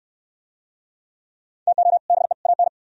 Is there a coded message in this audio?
Yes: Morse code from 1.7 to 2.7 s, heard as a keyed tone, about 67 wpm, 700 Hz. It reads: TO BE AN